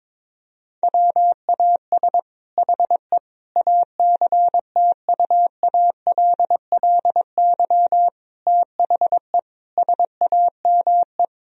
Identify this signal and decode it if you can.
Morse code, heard as a keyed tone, about 22 wpm, 700 Hz. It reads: WAS HE ACTUALLY THE SAME
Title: Sentences from Top 1000 Words - Review Part 4 - 22wpm